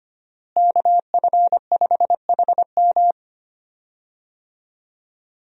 Morse code, heard as a keyed tone, about 25 wpm, 700 Hz. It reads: KF5HM
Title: USA Call Signs - 5 Characters - Repeated 3x - 25wpm